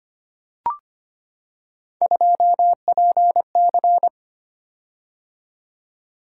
Morse code, heard as a keyed tone, about 25 wpm, 700 Hz. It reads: 2PC
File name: Three Letter-Number Combinations - 25wpm